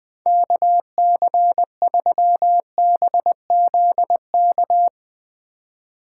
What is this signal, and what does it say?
Morse code, heard as a keyed tone, about 19 wpm, 700 Hz. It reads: KC3BZK
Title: Speed Racing - USA 2x3 Call Signs 17wpm